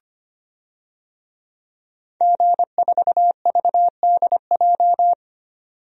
Morse code, heard as a keyed tone, about 25 wpm, 700 Hz. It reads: G4VDJ
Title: DXCC-100 Call Signs - 5 Characters - 25wpm